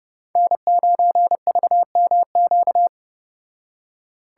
Morse code, heard as a keyed tone, about 30 wpm, 700 Hz. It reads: N9VMQ